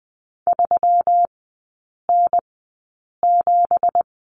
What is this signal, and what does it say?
Morse code, heard as a keyed tone, about 20 wpm, spaced as for 8 wpm, 690 Hz. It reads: 3N7
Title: 166 Lesson 27e - Number 7 - Three Characters 20wpm